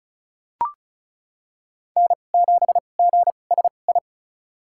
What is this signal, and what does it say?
Morse code, heard as a keyed tone, about 36 wpm, spaced as for 25 wpm, 700 Hz. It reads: N7GSI
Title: USA Call Signs - 2x Character Spacing - 35wpm